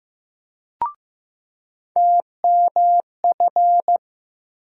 Morse code, heard as a keyed tone, about 15 wpm, 700 Hz. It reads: TMF
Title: USA Call Sign Postfixes - 3 Letters 15wpm